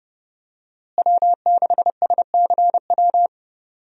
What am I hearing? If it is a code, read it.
Morse code, heard as a keyed tone, about 30 wpm, 700 Hz. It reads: W6SCW
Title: USA Call Signs - 5 Characters - Repeated 2x - 30wpm